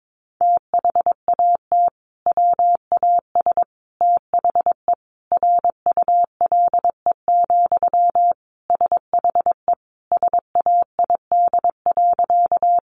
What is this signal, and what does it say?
Morse code, heard as a keyed tone, about 22 wpm, 700 Hz. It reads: THAT WAS THE RULE, SHE SAID.
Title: Sentences from Top 400 Words 22wpm